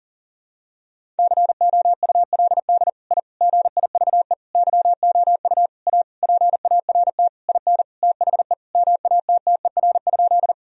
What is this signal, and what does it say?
Morse code, heard as a keyed tone, about 40 wpm, 700 Hz. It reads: COULD I GIVE YOU A PART IN THE MATTER?
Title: Sentences from Top 1000 Words - Review Part 1 - 40wpm